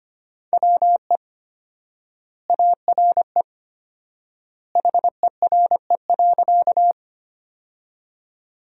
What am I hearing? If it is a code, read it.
Morse code, heard as a keyed tone, about 25 wpm, 700 Hz. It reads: WE ARE HERE.
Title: 4x Word-Spacing - Sentences from Top 200 Words - 25wpm